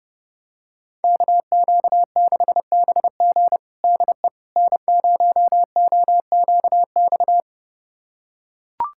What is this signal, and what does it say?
Morse code, heard as a keyed tone, about 30 wpm, 700 Hz. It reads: KQ6BG DE N0OQX
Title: USA Callsign DE Callsign - 5 Characters - 30wpm